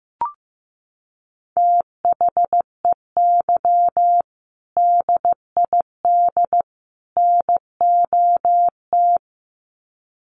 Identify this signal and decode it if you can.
Morse code, heard as a keyed tone, about 15 wpm, 700 Hz. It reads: THEY DID NOT
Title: Sets of 3 Words 15wpm